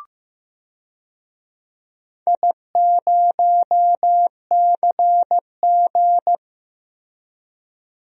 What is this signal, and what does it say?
Morse code, heard as a keyed tone, about 15 wpm, 700 Hz. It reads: I0CG
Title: DXCC-100 Call Signs - 4 Characters - Repeated 2x - 15wpm